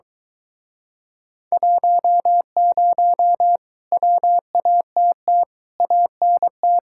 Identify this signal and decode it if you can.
Morse code, heard as a keyed tone, about 22 wpm, 700 Hz. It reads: 10 WATT ANT